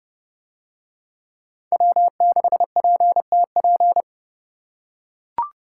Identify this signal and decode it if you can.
Morse code, heard as a keyed tone, about 28 wpm, 700 Hz. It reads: W6PTP